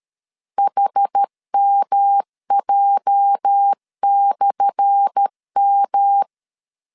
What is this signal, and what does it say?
Morse code, heard as a keyed tone, about 13 wpm, 800 Hz. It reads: HMJ/M
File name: LEZIONI / 10 SIGNS / 10A SLASH 12wpm 2314